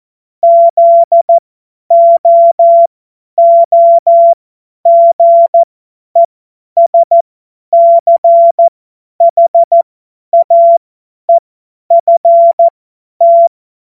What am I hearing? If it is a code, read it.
Morse code, heard as a keyed tone, about 14 wpm, spaced as for 10 wpm, 680 Hz. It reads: ZOOGESCHAEFT
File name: Morsekurs / Witze / Witz 02. Mai 2023 50 F70